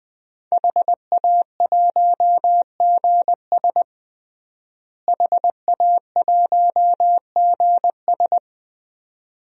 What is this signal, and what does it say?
Morse code, heard as a keyed tone, about 20 wpm, 700 Hz. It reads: HA1GS HA1GS
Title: DXCC-100 Call Signs - 5 Characters - Repeated 2x - 20wpm